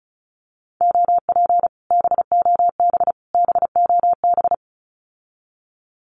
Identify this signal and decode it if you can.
Morse code, heard as a keyed tone, about 36 wpm, 700 Hz. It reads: OP BOB BOB
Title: Common QSO Phrases Encore 35wpm